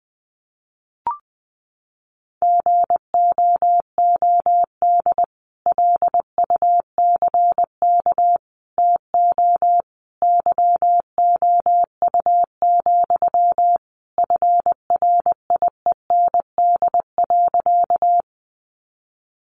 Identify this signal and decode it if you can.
Morse code, heard as a keyed tone, about 20 wpm, 700 Hz. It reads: GOOD LUCK TO YOU, FRIEND.
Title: Sentences from Top 1500 Words 20wpm